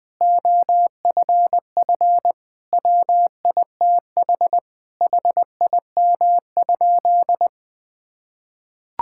Morse code, heard as a keyed tone, about 19 wpm, 700 Hz. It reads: OFF WITH HIM?
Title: Sentences from Top 1000 Words - Review Part 6 - 20wpm